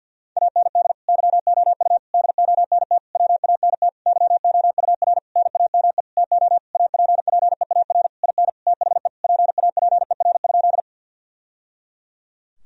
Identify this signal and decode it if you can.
Morse code, heard as a keyed tone, about 46 wpm, 700 Hz. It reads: AND YOU DONT WANT YOUR NAME TO APPEAR IN THE PAPER?
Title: Sentences from Top 1000 Words - Review Part 2 - 50wpm